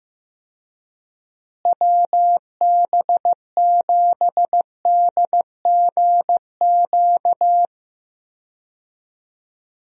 Morse code, heard as a keyed tone, about 16 wpm, 700 Hz. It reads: WB7DGQ
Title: USA Call Signs - 6 Characters - Repeated 2x - 15wpm